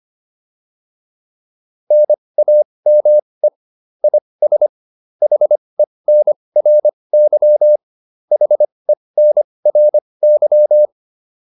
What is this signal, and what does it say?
Morse code, heard as a keyed tone, about 25 wpm, spaced as for 20 wpm, 600 Hz. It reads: NAME IS HENRY HENRY
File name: QSO102 20